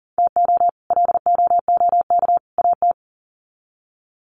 Morse code, heard as a keyed tone, about 38 wpm, 700 Hz. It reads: TO LOOK AT
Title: Sets of 3 Words 40wpm